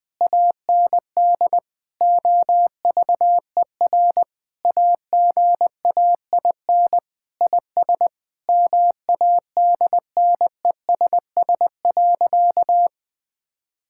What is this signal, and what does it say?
Morse code, heard as a keyed tone, about 20 wpm, 700 Hz. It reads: AND OVER AGAIN IS MADNESS.
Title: Sentences with 7-Word Phrases - 20wpm